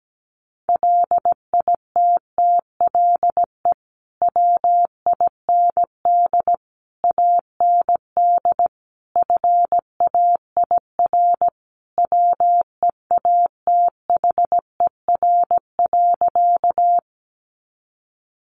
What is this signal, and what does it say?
Morse code, heard as a keyed tone, about 17 wpm, 700 Hz. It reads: LITTLE WIND AND FAIR WEATHER.
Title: Sentences from Top 1300 Words 17wpm